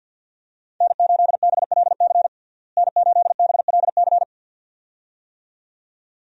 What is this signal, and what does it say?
Morse code, heard as a keyed tone, about 50 wpm, 700 Hz. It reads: N8BLC N8BLC
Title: USA Call Signs - 5 Characters - Repeated 2x - 50wpm